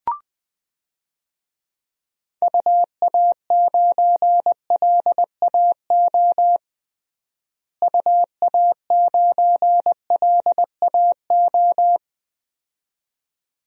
Morse code, heard as a keyed tone, about 21 wpm, 700 Hz. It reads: UA9LAO UA9LAO
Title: DXCC-100 Call Signs - 6 Characters - Repeated 2x - 20wpm